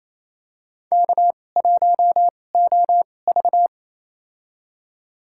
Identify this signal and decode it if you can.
Morse code, heard as a keyed tone, about 28 wpm, spaced as for 20 wpm, 700 Hz. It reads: K1OV
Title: USA Call Signs - 4 Characters - 2x Character Spacing - 28wpm